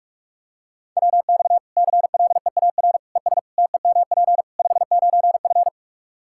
Morse code, heard as a keyed tone, about 46 wpm, 700 Hz. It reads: WX CLEAR ES TEMP 59F
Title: K1USN SST QSO Practice 45wpm